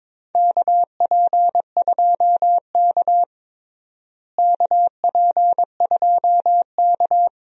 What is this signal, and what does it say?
Morse code, heard as a keyed tone, about 21 wpm, 700 Hz. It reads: KP2K KP2K
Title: DXCC-100 Call Signs - 4 Characters - Repeated 2x - 22wpm